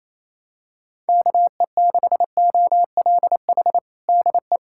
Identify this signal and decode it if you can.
Morse code, heard as a keyed tone, about 28 wpm, 700 Hz. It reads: KE6OLH DE
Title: USA Callsign DE Callsign - 6 Characters - 28wpm